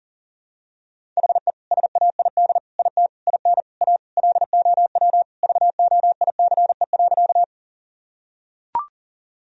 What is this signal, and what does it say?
Morse code, heard as a keyed tone, about 40 wpm, 700 Hz. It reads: HE SAID IT IN A LOW VOICE.
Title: Sentences from Top 1000 Words - Review Part 1 - 40wpm